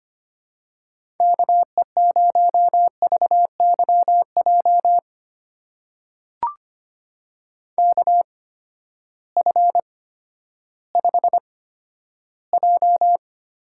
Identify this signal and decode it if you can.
Morse code, heard as a keyed tone, about 25 wpm, spaced as for 12 wpm, 700 Hz. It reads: KE0VYJ KF5J
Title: USA Call Signs - 6 Characters - 8x Character Spacing - 25wpm